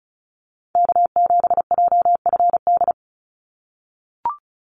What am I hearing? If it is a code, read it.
Morse code, heard as a keyed tone, about 34 wpm, 700 Hz. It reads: K7JFD
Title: NA SOTA Activator Callsigns 35wpm